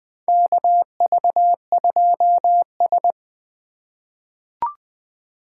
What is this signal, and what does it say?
Morse code, heard as a keyed tone, about 20 wpm, 700 Hz. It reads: KV2S
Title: USA Call Signs - 4 Characters - 2x Character Spacing - 20wpm